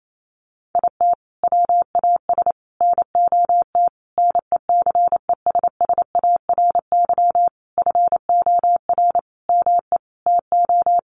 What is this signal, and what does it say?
Morse code, heard as a keyed tone, about 28 wpm, 700 Hz. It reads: IT WAS NOT NECESSARY FOR ME TO
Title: Sentences from Top 1300 Words 28wpm